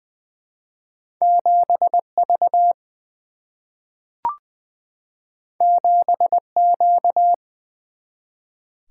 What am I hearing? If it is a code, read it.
Morse code, heard as a keyed tone, about 20 wpm, 700 Hz. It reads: 7 V 7 Q
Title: Single-Letter Single-Number 20wpm